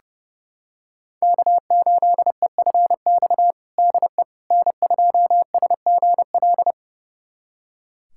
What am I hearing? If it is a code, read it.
Morse code, heard as a keyed tone, about 30 wpm, 700 Hz. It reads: K8EFX DE N2SGL
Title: USA Callsign DE Callsign - 5 Characters - 30wpm